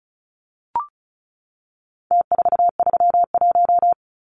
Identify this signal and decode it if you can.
Morse code, heard as a keyed tone, about 34 wpm, 700 Hz. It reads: T431
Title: Numbers - Contest Style - 4-Digit 35wpm